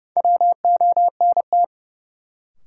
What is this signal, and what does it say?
Morse code, heard as a keyed tone, about 31 wpm, 700 Hz. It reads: WONT
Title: Sentences from Top 1000 Words - Review Part 6 - New Word Drill - 30wpm